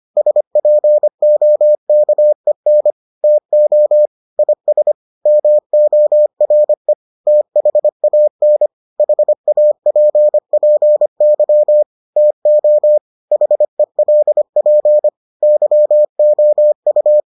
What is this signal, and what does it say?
Morse code, heard as a keyed tone, about 25 wpm, 600 Hz. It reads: SPOKEN TO IS MORE THAN HAPPY TO HELP YOU